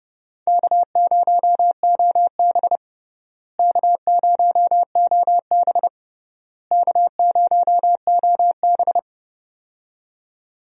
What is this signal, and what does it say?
Morse code, heard as a keyed tone, about 30 wpm, 700 Hz. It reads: K0OB K0OB K0OB
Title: USA Call Signs - 4 Characters - Repeated 3x - 30wpm